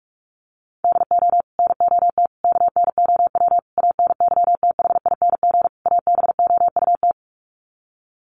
Morse code, heard as a keyed tone, about 44 wpm, 700 Hz. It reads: DO NOT KNOW ANYTHING ABOUT
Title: Sets of 5 Words 45wpm